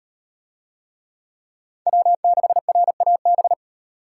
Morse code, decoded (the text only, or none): W6RAB